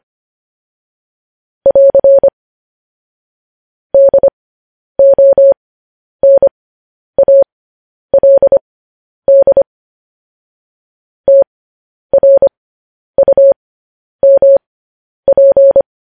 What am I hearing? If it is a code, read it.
Morse code, heard as a keyed tone, about 25 wpm, spaced as for 10 wpm, 550 Hz. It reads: <AR> DONALD TRUMP
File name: podcast / News headlines-1x-25x10